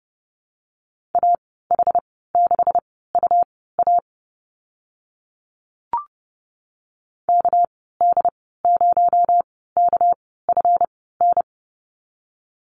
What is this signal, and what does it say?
Morse code, heard as a keyed tone, about 30 wpm, spaced as for 17 wpm, 700 Hz. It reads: AH6UA KD0KFN
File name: Lesson 37g - Letter x - Call Signs 30wpm